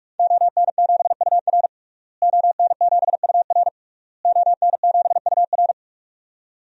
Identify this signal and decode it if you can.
Morse code, heard as a keyed tone, about 46 wpm, 700 Hz. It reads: ON7UR ON7UR ON7UR
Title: DXCC-100 Call Signs - 5 Characters - Repeated 3x - 45wpm